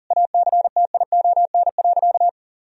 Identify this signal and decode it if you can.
Morse code, heard as a keyed tone, about 40 wpm, 700 Hz. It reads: ACTION.
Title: Sentences from Top 1000 Words - Review Part 4 - 40wpm